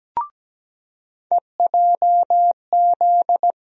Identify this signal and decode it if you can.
Morse code, heard as a keyed tone, about 17 wpm, 700 Hz. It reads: EJZ
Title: USA Call Sign Postfixes - 3 Letters 17wpm